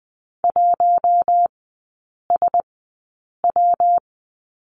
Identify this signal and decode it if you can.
Morse code, heard as a keyed tone, about 20 wpm, spaced as for 8 wpm, 700 Hz. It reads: 1SW
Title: 127 Lesson 21e - Number 6 - Three Characters 20wpm